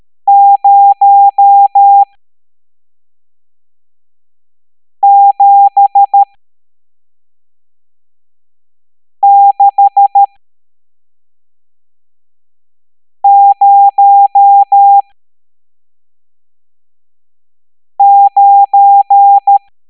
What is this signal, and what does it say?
Morse code, heard as a keyed tone, about 13 wpm, spaced as for 3 wpm, 800 Hz. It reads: 07609
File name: hamdata / L11-EX01-67890